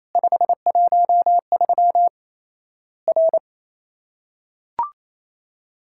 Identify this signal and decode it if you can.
Morse code, heard as a keyed tone, about 28 wpm, 700 Hz. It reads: 513 R
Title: Worked All Europe DX Contest - QTC Exchange - 28wpm